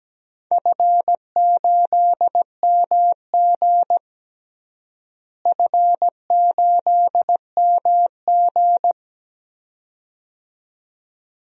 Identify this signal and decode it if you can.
Morse code, heard as a keyed tone, about 17 wpm, 700 Hz. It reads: F8MG F8MG
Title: DXCC-100 Call Signs - 4 Characters - Repeated 2x - 17wpm